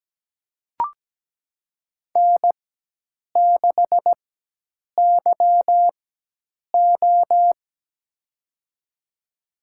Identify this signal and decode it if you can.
Morse code, heard as a keyed tone, about 17 wpm, spaced as for 8 wpm, 700 Hz. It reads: N6YO
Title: USA Call Signs - 4 Characters - 4x Character Spacing - 17wpm